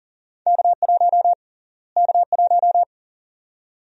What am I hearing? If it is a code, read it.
Morse code, heard as a keyed tone, about 37 wpm, 700 Hz. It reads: K1 K1
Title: USA Call Sign Prefixes - Repeated 2x - 40wpm